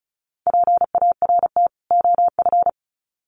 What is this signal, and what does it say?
Morse code, heard as a keyed tone, about 34 wpm, 700 Hz. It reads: PART OF
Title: Sets of 2 Words 35wpm